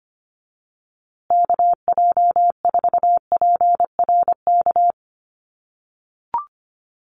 Morse code, heard as a keyed tone, about 25 wpm, 700 Hz. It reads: KJ4PRK
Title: USA Call Signs Repeated 3x 25wpm